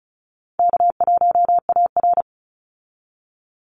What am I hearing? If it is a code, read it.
Morse code, heard as a keyed tone, about 34 wpm, 700 Hz. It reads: K1AR
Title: Easy Call Signs Repeated 2x - 35wpm